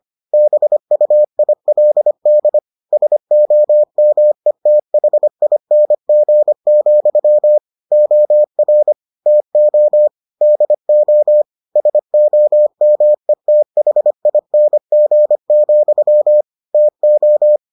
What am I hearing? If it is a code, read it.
Morse code, heard as a keyed tone, about 25 wpm, 600 Hz. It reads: BUILD SOMETHING, OR TO DO SOMETHING, TO